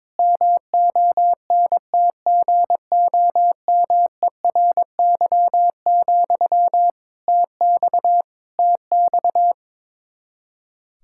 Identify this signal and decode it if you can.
Morse code, heard as a keyed tone, about 22 wpm, 700 Hz. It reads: MONTGOMERY, TX TX
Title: Practice QSO - Exchange 1 - Beginning - 22wpm